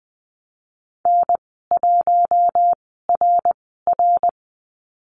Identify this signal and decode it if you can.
Morse code, heard as a keyed tone, about 20 wpm, spaced as for 15 wpm, 700 Hz. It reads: N1RR
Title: 069 Lesson 12g - Letter d - Call Signs 20wpm